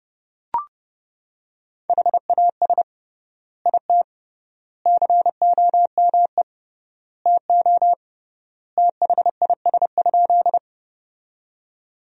Morse code, heard as a keyed tone, about 32 wpm, 700 Hz. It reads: HAS IT COME TO THIS?
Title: 3x Word-Spacing - Sentences from Top 100 Words - 30wpm